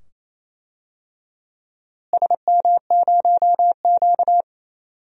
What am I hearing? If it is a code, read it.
Morse code, heard as a keyed tone, about 28 wpm, 700 Hz. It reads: SM0Q
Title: DXCC-100 Call Signs - 4 Characters - Repeated 2x - 28wpm